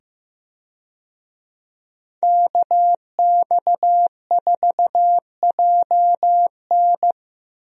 Morse code, heard as a keyed tone, about 15 wpm, 700 Hz. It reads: KX4JN
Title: USA Call Signs - 5 Characters - 15wpm